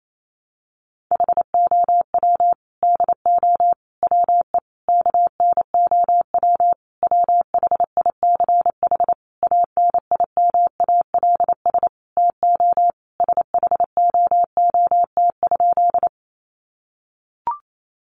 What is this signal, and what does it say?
Morse code, heard as a keyed tone, about 28 wpm, 700 Hz. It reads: HOW DO WE KNOW WHICH ANIMALS TO SHOOT?
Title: A Sound Of Thunder 28wpm